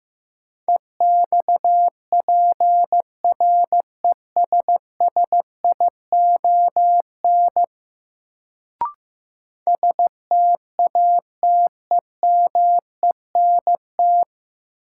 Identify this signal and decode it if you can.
Morse code, heard as a keyed tone, about 15 wpm, 700 Hz. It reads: EXPRESSION STATEMENT
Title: Sentences from Top 1200 Words - New Word Drill Alt - 15wpm